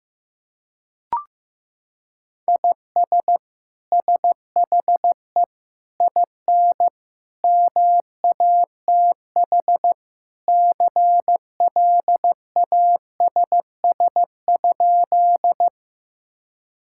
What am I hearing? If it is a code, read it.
Morse code, heard as a keyed tone, about 15 wpm, 700 Hz. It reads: IS SHE IN MATH CLASS?
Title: Sentences from Top 1000 Words - Review Part 6 - 15wpm